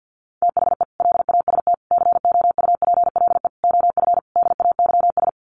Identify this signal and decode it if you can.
Morse code, heard as a keyed tone, about 50 wpm, 700 Hz. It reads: THE LAST COUPLE OF DAYS
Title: Sets of 5 Words Encore 50wpm